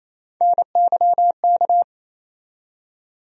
Morse code, heard as a keyed tone, about 28 wpm, 700 Hz. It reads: NYK
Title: USA Call Sign Postfixes - 3 Letters 28wpm